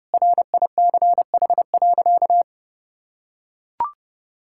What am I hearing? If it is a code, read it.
Morse code, heard as a keyed tone, about 30 wpm, 700 Hz. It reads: RICH.
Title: Sentences from Top 1600 Words 30wpm